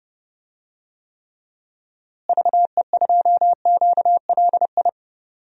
Morse code, heard as a keyed tone, about 30 wpm, 700 Hz. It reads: VE2QLI
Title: DXCC-100 Call Signs - 6 Characters - Repeated 3x - 30wpm